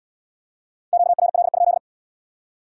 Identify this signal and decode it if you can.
Morse code, heard as a keyed tone, about 75 wpm, 700 Hz. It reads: QRL?